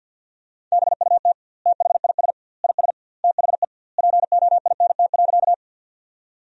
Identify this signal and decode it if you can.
Morse code, heard as a keyed tone, about 50 wpm, 700 Hz. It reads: BUT THIS IS THE POINT.
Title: 2x Word-Spacing - Sentences from Top 200 Words - 50wpm